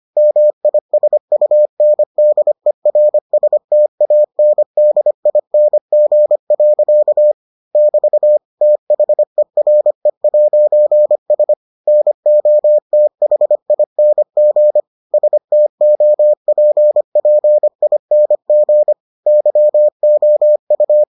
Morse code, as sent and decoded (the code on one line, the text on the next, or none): -- .. ... ..- -. -.. .-. ... - .- -. -.. .. -. --. .-.-.- / -...- / - .... .-. .----. ... / -. --- - .... .. -. --. / ... - --- .--. .--. .. -. --. / -.-- --- ..-
MISUNDERSTANDING. <BT> THERE'S NOTHING STOPPING YOU